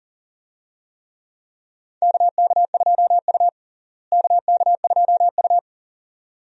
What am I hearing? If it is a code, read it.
Morse code, heard as a keyed tone, about 40 wpm, 700 Hz. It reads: KK2U KK2U
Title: USA Call Signs - 4 Characters - Repeated 2x - 40wpm